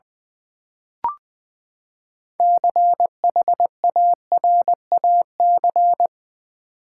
Morse code, heard as a keyed tone, about 20 wpm, 700 Hz. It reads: CHARAC
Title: Build A Word - 6 Letters At A Time - 12-Letter Words - 20wpm